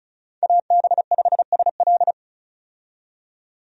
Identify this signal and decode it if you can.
Morse code, heard as a keyed tone, about 36 wpm, 700 Hz. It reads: AB5SL
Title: USA Call Signs - 5 Characters - 35wpm